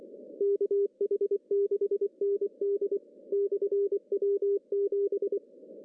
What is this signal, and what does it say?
Morse code, heard as a keyed tone, about 23 wpm, 400 Hz. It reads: KH6ND/W7
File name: CW / cqww-cw-k5tr